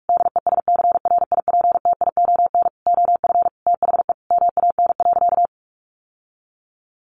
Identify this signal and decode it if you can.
Morse code, heard as a keyed tone, about 44 wpm, 700 Hz. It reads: DESCRIPTION OF THE MAN.
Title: Sentences from Top 1500 Words 45wpm